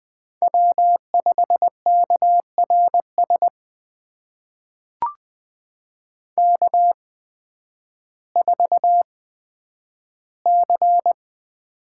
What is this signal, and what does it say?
Morse code, heard as a keyed tone, about 20 wpm, spaced as for 11 wpm, 700 Hz. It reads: W5KRS K4C
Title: USA Call Signs - 5 Characters - 8x Character Spacing - 20wpm